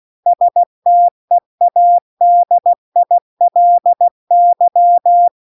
Morse code, heard as a keyed tone, about 16 wpm, 700 Hz. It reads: STEADILY